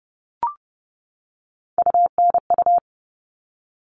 Morse code, heard as a keyed tone, about 30 wpm, 700 Hz. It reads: UNU
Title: All Cut Numbers - 3-Digit 30wpm